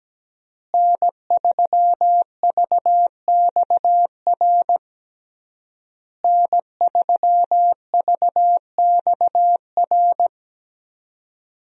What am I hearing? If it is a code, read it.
Morse code, heard as a keyed tone, about 17 wpm, 700 Hz. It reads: N3VXR N3VXR
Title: USA Call Signs - 5 Characters - Repeated 3x - 17wpm